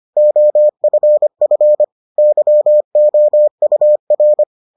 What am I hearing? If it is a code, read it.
Morse code, heard as a keyed tone, about 25 wpm, 600 Hz. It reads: OFF YOUR